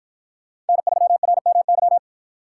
Morse code, heard as a keyed tone, about 53 wpm, 700 Hz. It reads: N3RMY